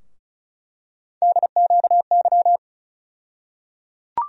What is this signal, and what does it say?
Morse code, heard as a keyed tone, about 34 wpm, 700 Hz. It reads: DQY
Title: USA Call Sign Postfixes - 3 Letters 35wpm